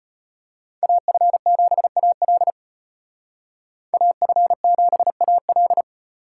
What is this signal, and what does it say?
Morse code, heard as a keyed tone, about 33 wpm, 700 Hz. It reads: AF7AL AF7AL